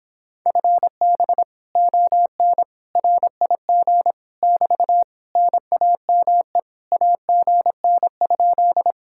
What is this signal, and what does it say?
Morse code, heard as a keyed tone, about 26 wpm, 700 Hz. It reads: FB ON RIG <BT> NAME AGN?